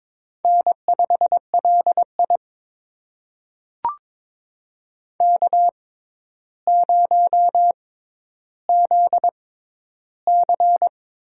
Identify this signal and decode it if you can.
Morse code, heard as a keyed tone, about 22 wpm, spaced as for 11 wpm, 700 Hz. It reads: N5LI K0ZC
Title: USA Call Signs - 6x Character Spacing - 22wpm